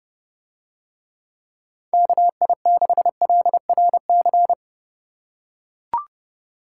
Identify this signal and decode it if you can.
Morse code, heard as a keyed tone, about 31 wpm, 700 Hz. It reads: KI6LRC